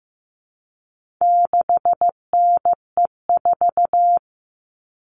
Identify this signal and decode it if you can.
Morse code, heard as a keyed tone, about 15 wpm, 700 Hz. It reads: 6NE4
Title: All Cut Numbers - 4-Digit 15wpm